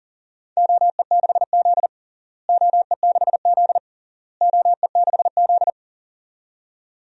Morse code, heard as a keyed tone, about 40 wpm, 700 Hz. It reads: OE6Z OE6Z OE6Z
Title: DXCC-100 Call Signs - 4 Characters - Repeated 3x - 40wpm